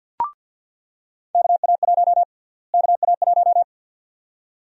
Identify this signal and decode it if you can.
Morse code, heard as a keyed tone, about 47 wpm, 700 Hz. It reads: KA1 KA1
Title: USA Call Sign Prefixes - Repeated 2x - 50wpm